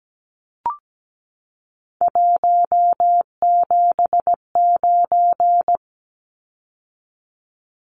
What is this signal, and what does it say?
Morse code, heard as a keyed tone, about 17 wpm, 700 Hz. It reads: 179
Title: Numbers - 3-Digit 17wpm